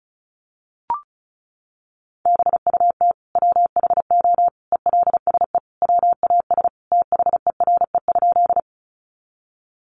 Morse code, heard as a keyed tone, about 34 wpm, 700 Hz. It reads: BUT WHO ELSE WAS THERE?
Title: Sentences from Top 800 Words 35wpm